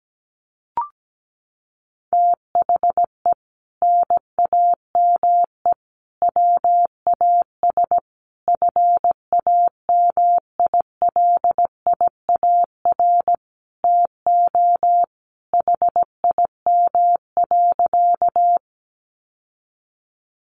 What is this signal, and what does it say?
Morse code, heard as a keyed tone, about 17 wpm, 700 Hz. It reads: THE NAME WAS FAMILIAR TO HIM.
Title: Sentences from Top 1000 Words 17wpm